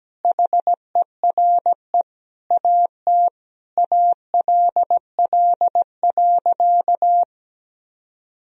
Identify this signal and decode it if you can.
Morse code, heard as a keyed tone, about 17 wpm, 700 Hz. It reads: HERE AT ALL.
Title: Sentences from Top 1000 Words - Review Part 1 - 17wpm